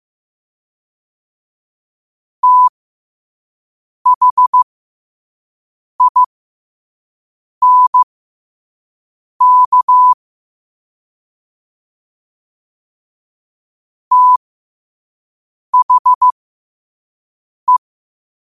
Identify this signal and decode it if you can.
Morse code, heard as a keyed tone, about 15 wpm, spaced as for 5 wpm, 1000 Hz. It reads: THINK THE